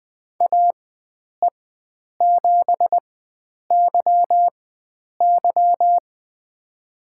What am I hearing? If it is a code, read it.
Morse code, heard as a keyed tone, about 20 wpm, spaced as for 9 wpm, 700 Hz. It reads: AE7YY
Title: USA Call Signs - 5 Characters - 4x Character Spacing - 20wpm